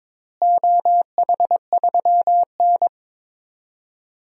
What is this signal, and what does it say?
Morse code, heard as a keyed tone, about 22 wpm, 700 Hz. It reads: OH3N
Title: DXCC-100 Call Signs - 4 Characters - Repeated 3x - 22wpm